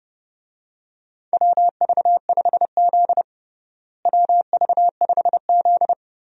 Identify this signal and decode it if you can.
Morse code, heard as a keyed tone, about 29 wpm, 700 Hz. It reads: WV5Z WV5Z
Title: USA Call Signs - 4 Characters - Repeated 3x - 30wpm